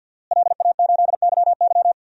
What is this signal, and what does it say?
Morse code, heard as a keyed tone, about 51 wpm, 700 Hz. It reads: LA8YY
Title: DXCC-100 Call Signs - 5 Characters - Repeated 3x - 50wpm